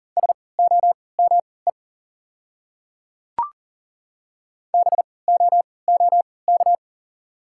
Morse code, heard as a keyed tone, about 40 wpm, spaced as for 23 wpm, 700 Hz. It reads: SOME BOOK
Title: 3x Character Spacing - Top 300 Words - 40wpm